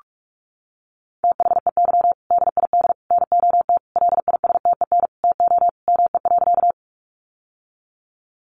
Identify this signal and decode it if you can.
Morse code, heard as a keyed tone, about 44 wpm, 700 Hz. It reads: THEY DID NOT LISTEN TO ME.
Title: Sentences from Top 600 Words 45wpm